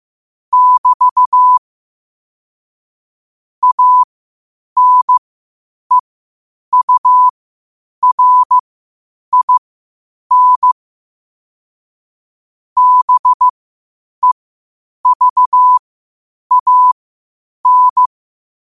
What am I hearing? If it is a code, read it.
Morse code, heard as a keyed tone, about 15 wpm, spaced as for 8 wpm, 1000 Hz. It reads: <BT> ANEURIN BEVAN
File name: qotdfiles / 8wpm